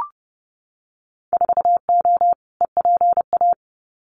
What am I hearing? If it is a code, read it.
Morse code, heard as a keyed tone, about 30 wpm, 700 Hz. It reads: 4O EPA
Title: Winter Field Day Exchanges 30wpm